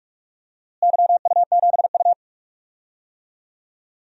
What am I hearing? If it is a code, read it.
Morse code, heard as a keyed tone, about 46 wpm, 700 Hz. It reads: YU7U
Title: DXCC-100 Call Signs - 4 Characters - Repeated 2x - 45wpm